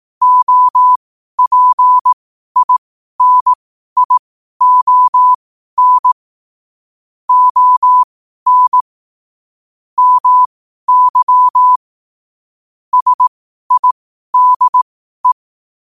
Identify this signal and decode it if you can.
Morse code, heard as a keyed tone, about 18 wpm, spaced as for 12 wpm, 1000 Hz. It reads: OPINION ON MY SIDE
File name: 12 wpm